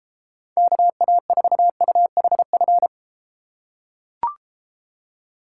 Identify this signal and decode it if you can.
Morse code, heard as a keyed tone, about 33 wpm, 700 Hz. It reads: KA4UHF